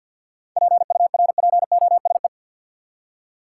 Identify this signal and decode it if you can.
Morse code, heard as a keyed tone, about 50 wpm, 700 Hz. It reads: PURPOSE
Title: Seven Letter Words - 50wpm